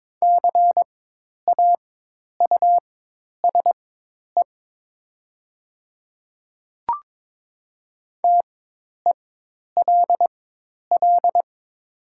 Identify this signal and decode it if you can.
Morse code, heard as a keyed tone, about 22 wpm, spaced as for 10 wpm, 700 Hz. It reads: CAUSE TELL
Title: 4x Character Spacing - Top 200 Words - 22wpm